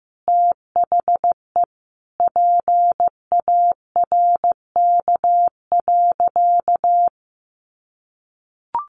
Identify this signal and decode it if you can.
Morse code, heard as a keyed tone, about 15 wpm, 700 Hz. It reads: THE PARK.
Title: Sentences from Top 1300 Words 15wpm